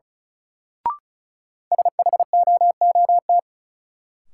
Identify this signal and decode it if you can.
Morse code, heard as a keyed tone, about 36 wpm, 700 Hz. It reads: SHOOT
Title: Sentences from Top 1200 Words - New Word Drill - 35wpm